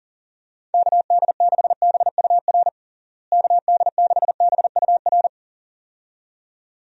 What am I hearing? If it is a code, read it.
Morse code, heard as a keyed tone, about 40 wpm, 700 Hz. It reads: KD6BUR KD6BUR
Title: USA Call Signs - 6 Characters - Repeated 2x - 40wpm